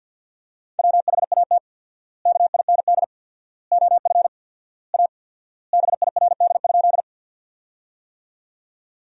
Morse code, heard as a keyed tone, about 50 wpm, 700 Hz. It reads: WHAT KIND OF A BIRD?
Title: 4x Word-Spacing - Sentences from Top 400 Words - 50wpm